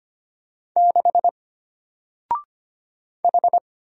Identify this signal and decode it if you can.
Morse code, heard as a keyed tone, about 25 wpm, 700 Hz. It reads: 6 H
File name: Single-Letter-Number-25wpm